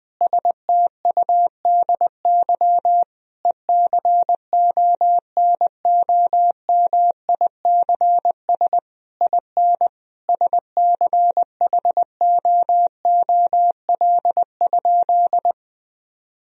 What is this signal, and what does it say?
Morse code, heard as a keyed tone, about 20 wpm, 700 Hz. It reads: STUDY ECONOMICS IN SCHOOL?
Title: Sentences from Top 1000 Words - Review Part 3 - 20wpm